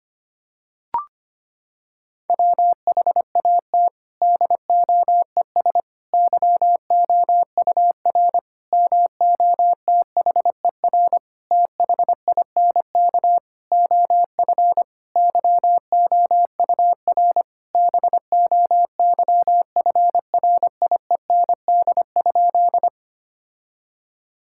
Morse code, heard as a keyed tone, about 25 wpm, 700 Hz. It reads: WHAT DOES YOUR MOTHER THINK OF YOUR BOYFRIEND?
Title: Sentences from Top 1000 Words - Review Part 5 - 25wpm